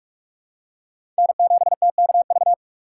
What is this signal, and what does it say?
Morse code, heard as a keyed tone, about 45 wpm, 700 Hz. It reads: N7TKV
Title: USA Call Signs - 5 Characters - 6x Character Spacing - 45wpm